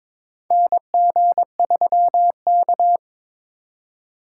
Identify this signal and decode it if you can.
Morse code, heard as a keyed tone, about 22 wpm, 700 Hz. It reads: NG3K
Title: USA Call Signs - 4 Characters - 4x Character Spacing - 22wpm